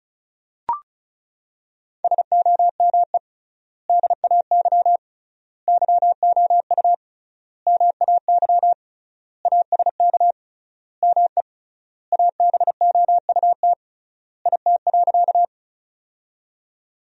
Morse code, heard as a keyed tone, about 34 wpm, 700 Hz. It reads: SOME DAY YOU MAY ASK ME ABOUT IT.
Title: 3x Word-Spacing - Sentences from Top 200 Words - 35wpm